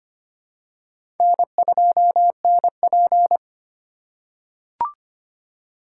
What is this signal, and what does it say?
Morse code, heard as a keyed tone, about 25 wpm, 700 Hz. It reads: N2NP
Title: USA Call Signs - 4 Characters - 3x Character Spacing - 25wpm